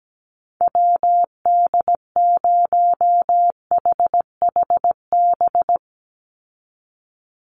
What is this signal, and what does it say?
Morse code, heard as a keyed tone, about 17 wpm, 700 Hz. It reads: WD0HHB
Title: USA Call Signs 17wpm